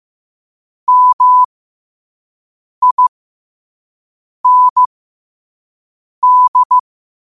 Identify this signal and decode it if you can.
Morse code, heard as a keyed tone, about 15 wpm, spaced as for 5 wpm, 1000 Hz. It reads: MIND
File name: qotdfiles / 5wpm